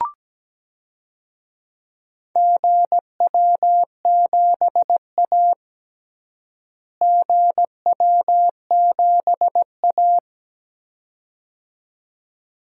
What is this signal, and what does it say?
Morse code, heard as a keyed tone, about 17 wpm, 700 Hz. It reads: GW7A GW7A
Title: DXCC-100 Call Signs - 4 Characters - Repeated 2x - 17wpm